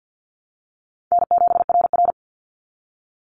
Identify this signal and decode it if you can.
Morse code, heard as a keyed tone, about 50 wpm, 700 Hz. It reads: N7RR
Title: NA SOTA Activator Callsigns 50wpm